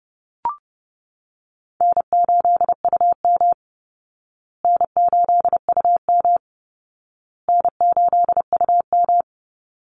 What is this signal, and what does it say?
Morse code, heard as a keyed tone, about 30 wpm, 700 Hz. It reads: N8UM N8UM N8UM
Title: Call Signs Repeated 2x - 30wpm